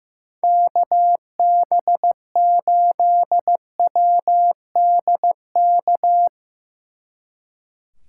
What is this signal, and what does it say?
Morse code, heard as a keyed tone, about 16 wpm, 700 Hz. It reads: KB8WDK
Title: Speed Racing - USA 2x3 Call Signs 15wpm